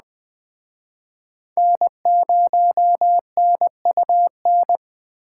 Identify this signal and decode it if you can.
Morse code, heard as a keyed tone, about 21 wpm, 700 Hz. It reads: N0NUN